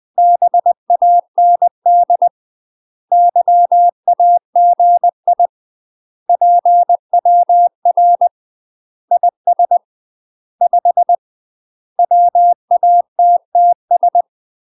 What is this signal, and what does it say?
Morse code, heard as a keyed tone, about 20 wpm, 700 Hz. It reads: BAND YAGI PWR IS 5 WATTS
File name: UK MASTER QSOs TRAINING / 20WPM / UK MASTER QSOs 0012